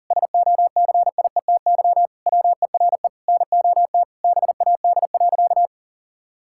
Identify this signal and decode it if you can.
Morse code, heard as a keyed tone, about 40 wpm, 700 Hz. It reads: SOCIETY WERE NOT BAD.
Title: Sentences from Top 1000 Words - Review Part 5 - 40wpm